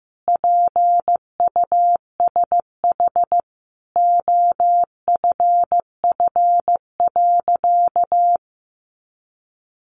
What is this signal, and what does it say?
Morse code, heard as a keyed tone, about 15 wpm, 700 Hz. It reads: PUSH OFF.
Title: Sentences from Top 1300 Words 15wpm